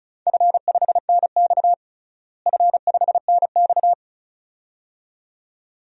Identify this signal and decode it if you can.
Morse code, heard as a keyed tone, about 34 wpm, 700 Hz. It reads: F5NX F5NX
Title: DXCC-100 Call Signs - 4 Characters - Repeated 2x - 35wpm